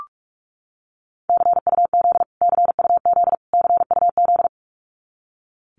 Morse code, heard as a keyed tone, about 44 wpm, 700 Hz. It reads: CUZ CUZ CUZ
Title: Abbreviations Repeated 3x - 45wpm